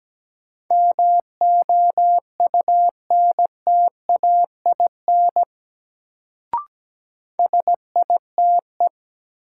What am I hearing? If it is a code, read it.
Morse code, heard as a keyed tone, about 17 wpm, 700 Hz. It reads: MOUNTAIN SITE
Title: Sentences from Top 1000 Words - Review Part 1 - New Word Drill - 17wpm